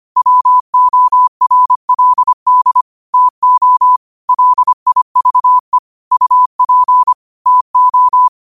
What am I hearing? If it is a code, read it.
Morse code, heard as a keyed tone, about 25 wpm, 1000 Hz. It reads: WORLD TO LIVE UP TO
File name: qotdfiles / 25wpm